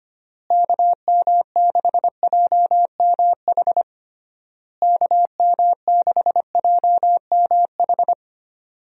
Morse code, heard as a keyed tone, about 25 wpm, 700 Hz. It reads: KM6JMH KM6JMH
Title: USA Call Signs - 6 Characters - Repeated 3x - 25wpm